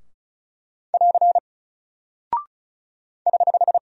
Morse code, heard as a keyed tone, about 36 wpm, 700 Hz. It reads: <AR> <HH>
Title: Single Character (Letter:Number:Punctuation:Prosign) Alt 35wpm